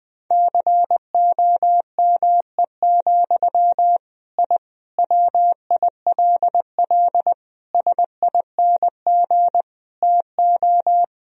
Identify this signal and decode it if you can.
Morse code, heard as a keyed tone, about 20 wpm, 700 Hz. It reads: COME, I WILL SING TO
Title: Sentences from Top 1000 Words - Review Part 2 - 20wpm